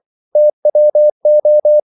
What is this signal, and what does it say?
Morse code, heard as a keyed tone, about 24 wpm, 600 Hz. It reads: TWO
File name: CW / Practice / If24